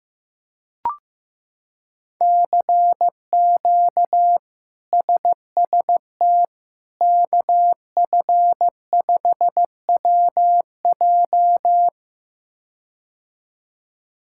Morse code, heard as a keyed tone, about 16 wpm, 700 Hz. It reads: CQ SST KF5WJ
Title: K1USN SST QSO Practice - 15wpm